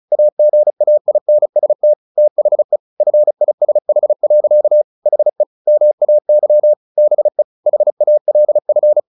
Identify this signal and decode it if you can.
Morse code, heard as a keyed tone, about 34 wpm, 600 Hz. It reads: AGAINST THE FISH. HE MAY BE HALF
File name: the old man and the sea 2 35wpm